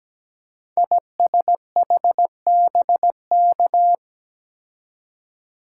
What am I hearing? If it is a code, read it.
Morse code, heard as a keyed tone, about 17 wpm, 700 Hz. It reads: ISHBK
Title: Code Groups - 5 Letters - 17wpm